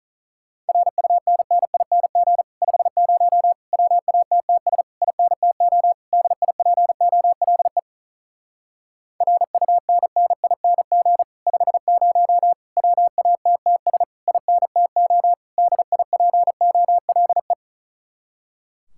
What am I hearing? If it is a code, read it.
Morse code, heard as a keyed tone, about 43 wpm, 700 Hz. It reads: RUNNING 50 WATTS INTO DIPOLE RUNNING 50 WATTS INTO DIPOLE
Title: Speed-Racing Common QSO Phrases - 35wpm